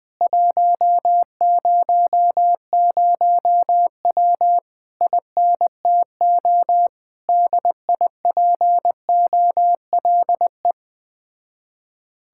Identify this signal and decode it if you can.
Morse code, heard as a keyed tone, about 20 wpm, 700 Hz. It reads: 100W INTO DIPOLE
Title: Speed-Racing Common QSO Phrases - 17wpm